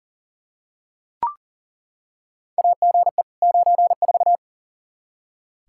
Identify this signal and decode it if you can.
Morse code, heard as a keyed tone, about 40 wpm, 700 Hz. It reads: AGE 94
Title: K1USN SST QSO Practice 40wpm